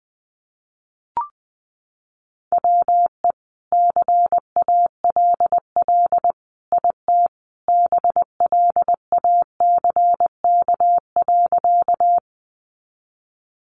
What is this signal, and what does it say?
Morse code, heard as a keyed tone, about 20 wpm, 700 Hz. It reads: WE CALL IT BLACK.
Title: Sentences from Top 400 Words 20wpm